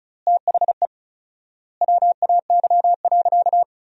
Morse code, heard as a keyed tone, about 36 wpm, 700 Hz. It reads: THE WAY.
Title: 4x Word-Spacing - Sentences from Top 300 Words - 35wpm